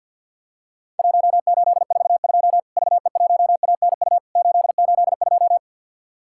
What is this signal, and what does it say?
Morse code, heard as a keyed tone, about 50 wpm, 700 Hz. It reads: 1842 VE1ANU 882
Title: Worked All Europe DX Contest - QTC Exchange - 50wpm